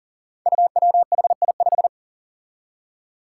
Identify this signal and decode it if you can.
Morse code, heard as a keyed tone, about 40 wpm, 700 Hz. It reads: UWHI5
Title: Five Challenging Letter-Number Combinations 40wpm